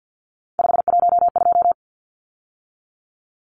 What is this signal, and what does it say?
Morse code, heard as a keyed tone, about 49 wpm, 700 Hz. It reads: 512
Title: Numbers - 3-Digit 50wpm